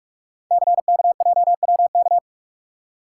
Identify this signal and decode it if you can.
Morse code, heard as a keyed tone, about 44 wpm, 700 Hz. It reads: CKJWK
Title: Code Groups - 5 Letters - 45wpm